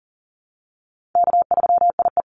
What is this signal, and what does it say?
Morse code, heard as a keyed tone, about 40 wpm, 700 Hz. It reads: K3IE
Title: Easy Call Signs Repeated 2x - 40wpm